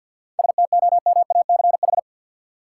Morse code, heard as a keyed tone, about 50 wpm, 700 Hz. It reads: STOMACH
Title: Seven Letter Words - 50wpm